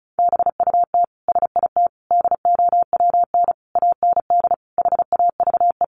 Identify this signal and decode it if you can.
Morse code, heard as a keyed tone, about 34 wpm, 700 Hz. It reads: BUT SIT DOWN AND HAVE
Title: Sentences from Top 600 Words 35wpm